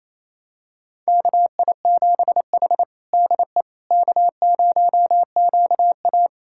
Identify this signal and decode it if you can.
Morse code, heard as a keyed tone, about 28 wpm, 700 Hz. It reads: KI7H DE K0QA
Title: USA Callsign DE Callsign - 4 Characters - 28wpm